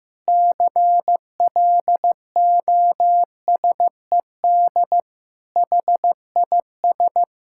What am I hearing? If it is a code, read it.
Morse code, heard as a keyed tone, about 15 wpm, 700 Hz. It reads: CLOSED HIS
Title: Sentences from Top 1600 Words 15wpm